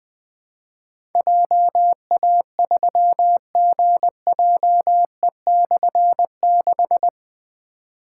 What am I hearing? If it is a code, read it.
Morse code, heard as a keyed tone, about 20 wpm, 700 Hz. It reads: JA3GJE/6
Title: Difficult Call Signs 20wpm